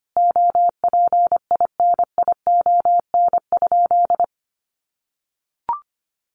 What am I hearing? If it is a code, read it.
Morse code, heard as a keyed tone, about 25 wpm, 700 Hz. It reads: OPINION?
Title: Sentences from Top 1200 Words 25wpm